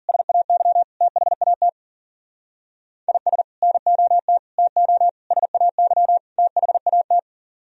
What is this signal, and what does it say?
Morse code, heard as a keyed tone, about 52 wpm, 700 Hz. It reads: SAY THAT IS NOT TO SAY THAT